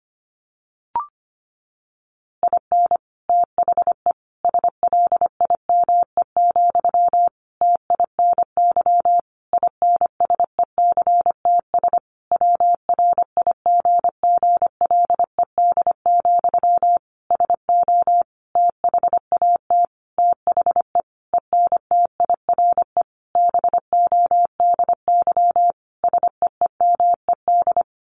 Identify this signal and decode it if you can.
Morse code, heard as a keyed tone, about 25 wpm, 700 Hz. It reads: IN THE SLIME, TINY INSECTS WRIGGLED, SO THAT THE ENTIRE BODY SEEMED
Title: A Sound Of Thunder 25wpm